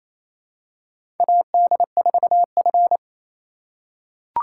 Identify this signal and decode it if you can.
Morse code, heard as a keyed tone, about 28 wpm, 700 Hz. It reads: AD4F
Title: USA Call Signs - 2x Character Spacing - 28wpm